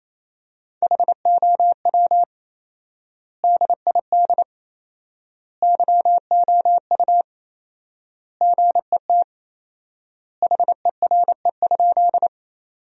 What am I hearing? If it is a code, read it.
Morse code, heard as a keyed tone, about 28 wpm, 700 Hz. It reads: HOW DID YOU GET HERE?
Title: 4x Word-Spacing - Sentences from Top 200 Words - 28wpm